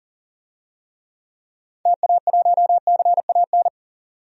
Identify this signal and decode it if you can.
Morse code, heard as a keyed tone, about 40 wpm, 700 Hz. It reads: TA1CAN
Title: DXCC-100 Call Signs - 6 Characters - 40wpm